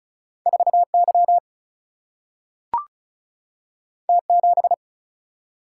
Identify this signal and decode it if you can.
Morse code, heard as a keyed tone, about 36 wpm, 700 Hz. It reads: 4 Y T 7
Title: Single-Letter Single-Number 35wpm